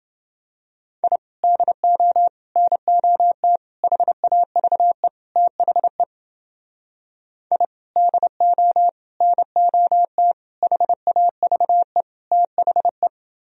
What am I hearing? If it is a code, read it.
Morse code, heard as a keyed tone, about 30 wpm, 700 Hz. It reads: I DO NOT HAVE THE I DO NOT HAVE THE